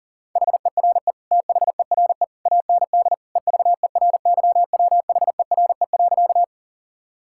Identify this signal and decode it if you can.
Morse code, heard as a keyed tone, about 40 wpm, 700 Hz. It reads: HERE THERE AND EVERYWHERE.
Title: Sentences from Top 1600 Words 40wpm